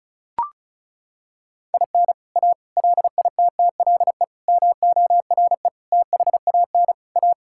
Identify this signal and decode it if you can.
Morse code, heard as a keyed tone, about 36 wpm, 700 Hz. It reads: IN A LITTLE MORE THAN A
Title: Sentences from Top 1000 Words - Review Part 5 - 35wpm